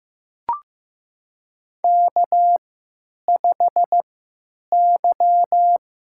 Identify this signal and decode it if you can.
Morse code, heard as a keyed tone, about 15 wpm, spaced as for 9 wpm, 700 Hz. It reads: K5Y
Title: USA Call Signs - 3x Character Spacing - 15wpm